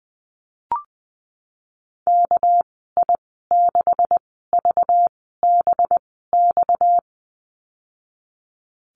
Morse code, heard as a keyed tone, about 19 wpm, spaced as for 14 wpm, 700 Hz. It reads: KI6VBX
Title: 231 Lesson 37g - Letter x - Call Signs 20wpm